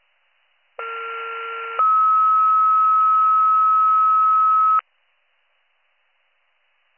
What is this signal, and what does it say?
Two-tone page, heard as a one-second tone, then a three-second tone, about 1370 then 1290 Hz. It is heard from 0.8 to 4.8 s.